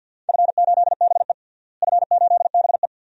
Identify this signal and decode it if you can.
Morse code, heard as a keyed tone, about 52 wpm, 700 Hz. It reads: F8BE F8BE
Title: DXCC-100 Call Signs - 4 Characters - Repeated 2x - 50wpm